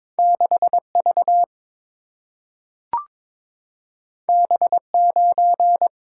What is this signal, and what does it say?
Morse code, heard as a keyed tone, about 22 wpm, 700 Hz. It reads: 6V B9
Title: Single-Letter Single-Number 22wpm